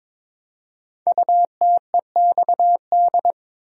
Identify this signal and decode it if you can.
Morse code, heard as a keyed tone, about 22 wpm, 700 Hz. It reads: UTEXD
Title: Code Groups - 5 Letters - 22wpm